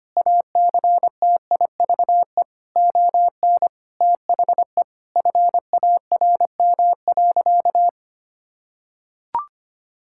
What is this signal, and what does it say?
Morse code, heard as a keyed tone, about 25 wpm, 700 Hz. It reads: ACTIVE ON THE FARM.
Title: Sentences from Top 1000 Words - Review Part 6 - 25wpm